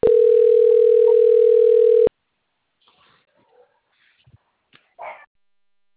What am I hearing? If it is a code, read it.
Ringback tone (North America), 440 + 480 Hz.
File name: rg-1600-unknown- - - .19631